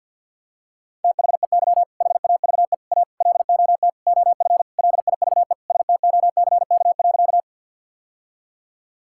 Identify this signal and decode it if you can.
Morse code, heard as a keyed tone, about 50 wpm, 700 Hz. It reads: THEY HAVE A LOT OF LIVESTOCK.
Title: Sentences from Top 1000 Words - Review Part 5 - 50wpm